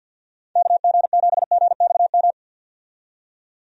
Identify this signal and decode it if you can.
Morse code, heard as a keyed tone, about 49 wpm, 700 Hz. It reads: KG7GXM
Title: USA Call Signs - 6 Characters - 50wpm